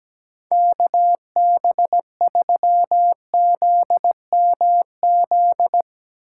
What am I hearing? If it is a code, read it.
Morse code, heard as a keyed tone, about 17 wpm, 700 Hz. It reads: KB3ZMZ
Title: Speed Racing - USA 2x3 Call Signs 17wpm